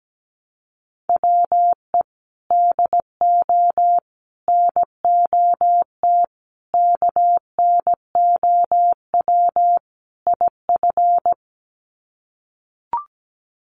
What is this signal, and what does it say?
Morse code, heard as a keyed tone, about 17 wpm, 700 Hz. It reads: WE DO NOT KNOW IF
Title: Sets of 5 Words Encore2 17wpm